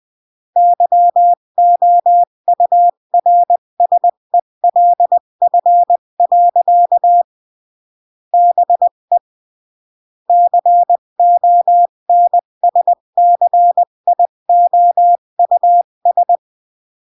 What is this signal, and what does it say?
Morse code, heard as a keyed tone, about 20 wpm, 700 Hz. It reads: YOURSELF. BE CONSCIOUS
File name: CW / jodorowsky / 82 commandments 20-15